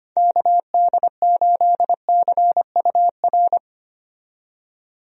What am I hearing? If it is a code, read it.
Morse code, heard as a keyed tone, about 25 wpm, 700 Hz. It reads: KD8CUR
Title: USA Call Signs - 6 Characters - 25wpm